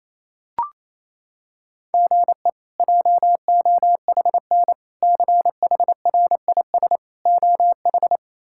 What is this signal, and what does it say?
Morse code, heard as a keyed tone, about 28 wpm, 700 Hz. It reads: GE JOHN CHRIS OH
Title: K1USN SST QSO Practice - 28wpm